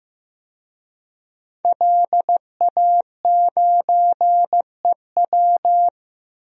Morse code, heard as a keyed tone, about 15 wpm, 700 Hz. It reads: LA9EW
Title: DXCC-100 Call Signs - 5 Characters - Repeated 2x - 15wpm